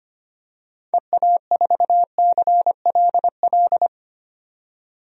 Morse code, heard as a keyed tone, about 25 wpm, 700 Hz. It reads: EA4CLL
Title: DXCC-100 Call Signs - 6 Characters - 25wpm